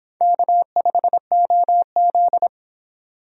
Morse code, heard as a keyed tone, about 28 wpm, 700 Hz. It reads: K5OZ